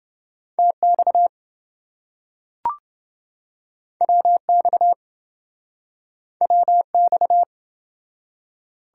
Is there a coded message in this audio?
Yes: Morse code from 0.6 to 7.4 s, heard as a keyed tone, about 26 wpm, 700 Hz. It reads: TX WX WX